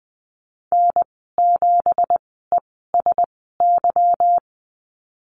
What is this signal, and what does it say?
Morse code, heard as a keyed tone, about 20 wpm, spaced as for 15 wpm, 700 Hz. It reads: N7ESY
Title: 186 Lesson 30g - Letter b - Call Signs 20wpm